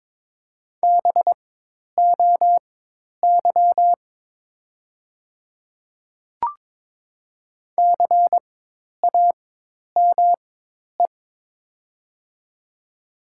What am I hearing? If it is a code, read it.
Morse code, heard as a keyed tone, about 22 wpm, spaced as for 10 wpm, 700 Hz. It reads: BOY CAME
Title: 4x Character Spacing - Top 200 Words - 22wpm